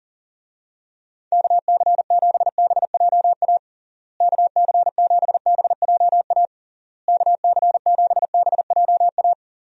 Morse code, heard as a keyed tone, about 37 wpm, 700 Hz. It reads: KC7BJA KC7BJA KC7BJA
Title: USA Call Signs - 6 Characters - Repeated 3x - 40wpm